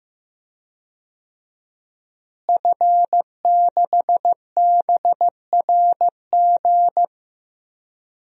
Morse code, heard as a keyed tone, about 14 wpm, 700 Hz. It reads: F6BRG
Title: DXCC-100 Call Signs - 5 Characters - Repeated 2x - 15wpm